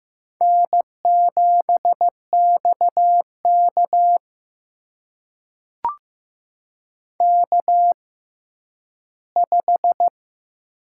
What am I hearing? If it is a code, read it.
Morse code, heard as a keyed tone, about 15 wpm, spaced as for 10 wpm, 700 Hz. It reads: N7XK K5
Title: USA Call Signs - 4 Characters - 6x Character Spacing - 15wpm